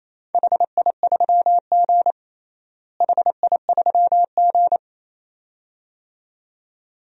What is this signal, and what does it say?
Morse code, heard as a keyed tone, about 28 wpm, 700 Hz. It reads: HI3G HI3G
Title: DXCC-100 Call Signs - 4 Characters - Repeated 2x - 28wpm